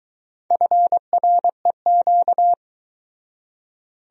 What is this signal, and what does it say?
Morse code, heard as a keyed tone, about 23 wpm, 700 Hz. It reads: FREQ